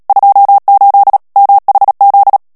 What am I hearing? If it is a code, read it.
Morse code, heard as a keyed tone, about 37 wpm, 800 Hz. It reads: 28 MHZ
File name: morse mp3 / ANGLAIS texte clair 3 vitesse 35 mots minute